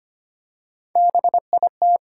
Morse code, heard as a keyed tone, about 25 wpm, 700 Hz. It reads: BIT
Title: Three Letter Words - 25wpm